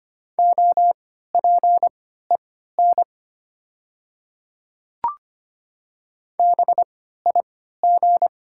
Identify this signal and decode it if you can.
Morse code, heard as a keyed tone, about 25 wpm, spaced as for 14 wpm, 700 Hz. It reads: OPEN BIG
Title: 3x Character Spacing - Top 300 Words - 25wpm